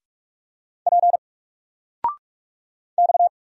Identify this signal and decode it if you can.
Morse code, heard as a keyed tone, about 46 wpm, 700 Hz. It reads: P X
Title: Single Letter-Number 45wpm